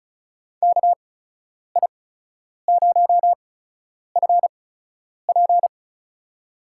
Morse code, heard as a keyed tone, about 36 wpm, spaced as for 10 wpm, 700 Hz. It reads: KI0FP
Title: USA Call Signs - 8x Character Spacing - 35wpm